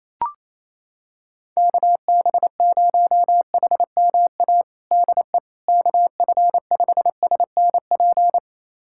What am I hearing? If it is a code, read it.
Morse code, heard as a keyed tone, about 28 wpm, 700 Hz. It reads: KB0HMA DE KF5SNP
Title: USA Callsign DE Callsign - 6 Characters - 28wpm